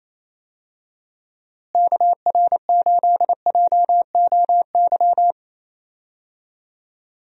Morse code, heard as a keyed tone, about 28 wpm, 700 Hz. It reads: KR8JOY
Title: USA Call Signs - 6 Characters - Repeated 3x - 28wpm